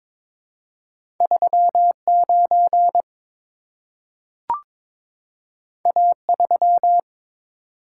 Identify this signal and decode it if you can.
Morse code, heard as a keyed tone, about 22 wpm, 700 Hz. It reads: 39 A3
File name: practice / 22wpm / Two Letter-Number Combinations - 22wpm